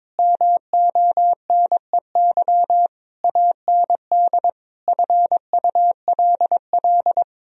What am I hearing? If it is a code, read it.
Morse code, heard as a keyed tone, about 21 wpm, 700 Hz. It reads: MONEY AND FULL
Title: Sentences from Top 1000 Words - Review Part 3 - 22wpm